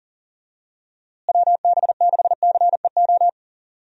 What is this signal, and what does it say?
Morse code, heard as a keyed tone, about 37 wpm, 700 Hz. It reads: WB6CEO
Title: USA Call Signs - 6 Characters - Repeated 3x - 40wpm